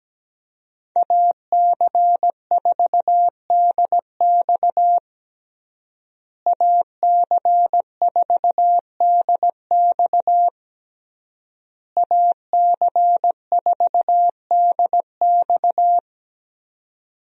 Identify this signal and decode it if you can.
Morse code, heard as a keyed tone, about 17 wpm, 700 Hz. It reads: AC4DX AC4DX AC4DX
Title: USA Call Signs - 5 Characters - Repeated 3x - 17wpm